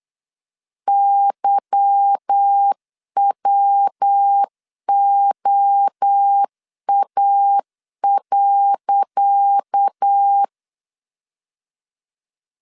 Morse code, heard as a keyed tone, about 8 wpm, 800 Hz. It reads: YWOA.
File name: LEZIONI / 10 SIGNS / 10C DOT 8wpm 3649